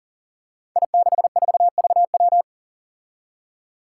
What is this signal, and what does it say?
Morse code, heard as a keyed tone, about 37 wpm, 700 Hz. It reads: I64VW
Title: Five Challenging Letter-Number Combinations 40wpm